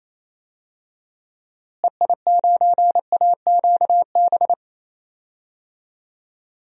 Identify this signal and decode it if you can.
Morse code, heard as a keyed tone, about 28 wpm, 700 Hz. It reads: EI9AQB
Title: DXCC-100 Call Signs - 6 Characters - Repeated 2x - 28wpm